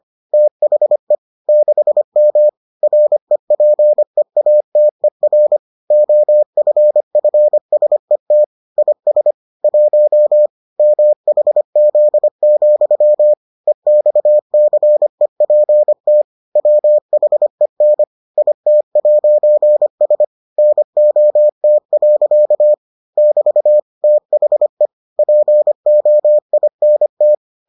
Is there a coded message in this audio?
Yes: Morse code from 0.3 to 27.3 s, heard as a keyed tone, about 25 wpm, 600 Hz. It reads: THE 6M REPEATER OFFSET IS 1 MHZ, EXCEPT WHEN IT'S NOT. <BT> THE POINT